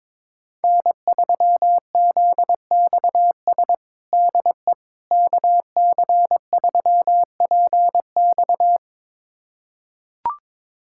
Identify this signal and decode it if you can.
Morse code, heard as a keyed tone, about 22 wpm, 700 Hz. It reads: N3ZXS DE KC3PX
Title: USA Callsign DE Callsign - 5 Characters - 22wpm